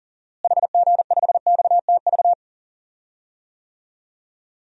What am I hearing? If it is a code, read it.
Morse code, heard as a keyed tone, about 40 wpm, 700 Hz. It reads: HG5XTV
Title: DXCC-100 Call Signs - 6 Characters - Repeated 2x - 40wpm